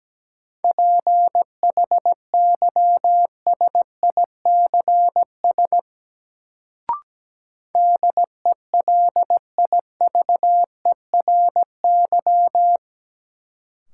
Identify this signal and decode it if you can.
Morse code, heard as a keyed tone, about 17 wpm, 700 Hz. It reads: PHYSICS DELIVERY
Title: Sentences from Top 1500 Words - New Word Drill Alt - 17wpm